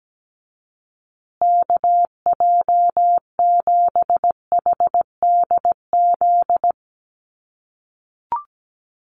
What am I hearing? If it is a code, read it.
Morse code, heard as a keyed tone, about 17 wpm, 700 Hz. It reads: KJ7HDZ
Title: USA Call Signs Repeated 3x 17wpm